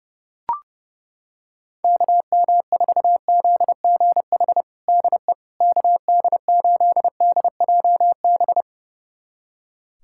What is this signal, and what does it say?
Morse code, heard as a keyed tone, about 30 wpm, 700 Hz. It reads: KM4ZGH DE KD8DJB
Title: USA Callsign DE Callsign 30wpm